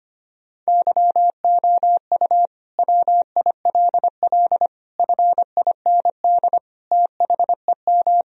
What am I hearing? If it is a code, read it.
Morse code, heard as a keyed tone, about 25 wpm, 700 Hz. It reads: YOU WILL FIND THEM
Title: Sentences from Top 1000 Words - Review Part 4 - 25wpm